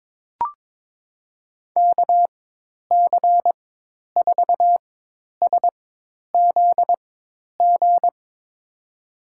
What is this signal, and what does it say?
Morse code, heard as a keyed tone, about 22 wpm, spaced as for 10 wpm, 700 Hz. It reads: KC4SZG
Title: USA Call Signs - 6 Characters - 4x Character Spacing - 22wpm